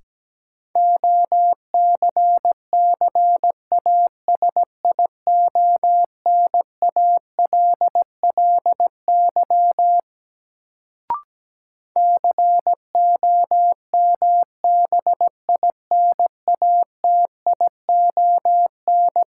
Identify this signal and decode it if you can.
Morse code, heard as a keyed tone, about 17 wpm, 700 Hz. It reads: OCCASIONALLY COMBINATION
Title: Sentences from Top 1300 Words - New Word Drill Alt - 17wpm